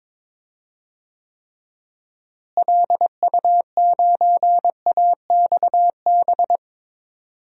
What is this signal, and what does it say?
Morse code, heard as a keyed tone, about 22 wpm, 700 Hz. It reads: LU9AXB
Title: DXCC-100 Call Signs - 6 Characters - Repeated 2x - 22wpm